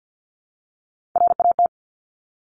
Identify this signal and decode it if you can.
Morse code, heard as a keyed tone, about 50 wpm, 700 Hz. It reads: RAT
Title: Top Three Letter Combinations in Words 50wpm